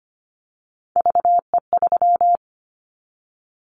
Morse code, heard as a keyed tone, about 25 wpm, 700 Hz. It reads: VE3
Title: Callsign Prefixes for Top 25 DXCC - Repeated 3x - 25wpm